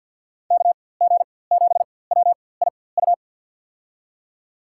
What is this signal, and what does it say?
Morse code, heard as a keyed tone, about 48 wpm, spaced as for 23 wpm, 700 Hz. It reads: KG7WIU
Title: USA Call Signs - 6 Characters - 4x Character Spacing - 50wpm